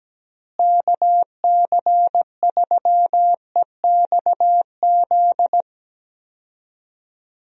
Morse code, heard as a keyed tone, about 17 wpm, 700 Hz. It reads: KC3EXZ
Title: USA Call Signs - 6 Characters - Repeated 3x - 17wpm